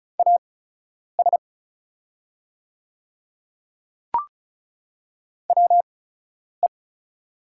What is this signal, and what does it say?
Morse code, heard as a keyed tone, about 36 wpm, spaced as for 10 wpm, 700 Hz. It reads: AS WE
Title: 8x Character Spacing - Top 100 Words - 35wpm